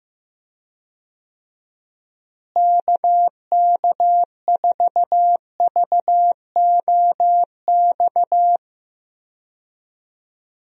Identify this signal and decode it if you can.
Morse code, heard as a keyed tone, about 15 wpm, 700 Hz. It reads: KK4VOX
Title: USA Call Signs - 6 Characters - Repeated 2x - 15wpm